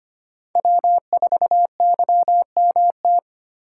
Morse code, heard as a keyed tone, about 25 wpm, 700 Hz. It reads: W4YMT
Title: USA Call Signs - 5 Characters - 4x Character Spacing - 25wpm